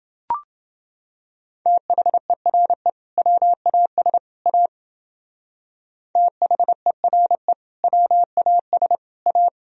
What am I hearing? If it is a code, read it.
Morse code, heard as a keyed tone, about 29 wpm, 700 Hz. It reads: THERE WAS A THERE WAS A